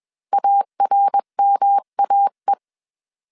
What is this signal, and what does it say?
Morse code, heard as a keyed tone, about 21 wpm, 800 Hz. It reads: ARMAE
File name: 20wpm